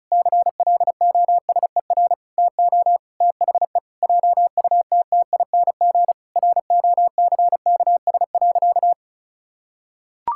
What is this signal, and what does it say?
Morse code, heard as a keyed tone, about 34 wpm, 700 Hz. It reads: CLOSER TO THE JUTTING ROCKS.
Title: Pangram Challenge v2 - 35wpm